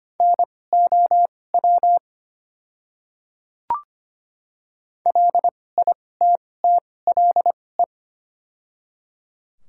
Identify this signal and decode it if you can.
Morse code, heard as a keyed tone, about 25 wpm, spaced as for 18 wpm, 700 Hz. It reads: NOW LITTLE
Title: 2x Character Spacing - Top 300 Words - 25wpm